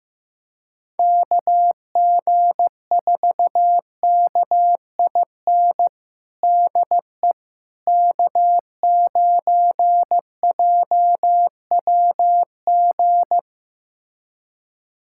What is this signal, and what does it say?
Morse code, heard as a keyed tone, about 15 wpm, 700 Hz. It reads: KG4KIN DE K9JWG
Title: USA Callsign DE Callsign 15wpm